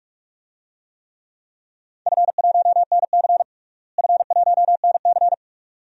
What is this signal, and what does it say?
Morse code, heard as a keyed tone, about 46 wpm, 700 Hz. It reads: F1NC F1NC
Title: DXCC-100 Call Signs - 4 Characters - Repeated 2x - 45wpm